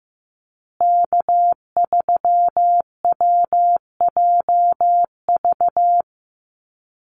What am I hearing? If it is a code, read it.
Morse code, heard as a keyed tone, about 15 wpm, 700 Hz. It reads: K3WJV
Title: Call Signs Repeated 3x - 15wpm